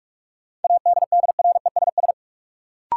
Morse code, heard as a keyed tone, about 42 wpm, 700 Hz. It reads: ADDRESS
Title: Sets of 5 Words Encore - New Word Drill - 45wpm